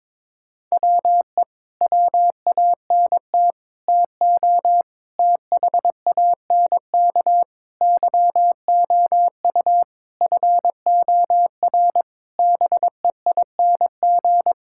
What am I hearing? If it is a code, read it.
Morse code, heard as a keyed tone, about 22 wpm, 700 Hz. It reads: WE WANT TO THANK YOU FOR BEING
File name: Sets of 9 Words - 22wpm